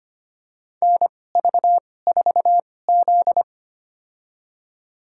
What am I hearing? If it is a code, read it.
Morse code, heard as a keyed tone, about 25 wpm, spaced as for 18 wpm, 700 Hz. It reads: NV4Z
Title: USA Call Signs - 4 Characters - 2x Character Spacing - 25wpm